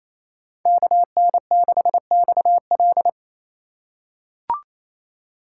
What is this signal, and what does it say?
Morse code, heard as a keyed tone, about 27 wpm, 700 Hz. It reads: KN6XL
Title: USA Call Signs - 5 Characters - 2x Character Spacing - 28wpm